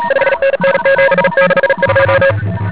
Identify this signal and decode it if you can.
Morse code, heard as a keyed tone, about 40 wpm, 530 Hz, with noise. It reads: 5NN 762
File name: ARCHIV / 40m cw1